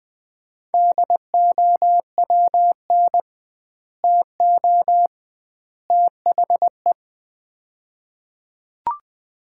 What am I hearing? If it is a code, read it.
Morse code, heard as a keyed tone, about 20 wpm, 700 Hz. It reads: DOWN TO THE
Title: 2x Word-Spacing - Sets of 3 Words - 20wpm